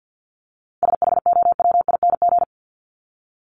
Morse code, heard as a keyed tone, about 52 wpm, 700 Hz. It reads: SHOWING
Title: ING Suffix - Top 100 Words with Base Words 50wpm